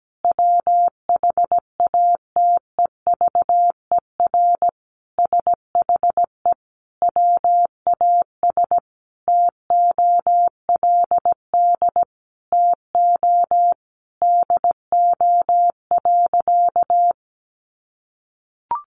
Morse code, heard as a keyed tone, about 17 wpm, 700 Hz. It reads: WHATEVER SHE WAS TOLD TO DO.
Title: Sentences from Top 1100 Words 17wpm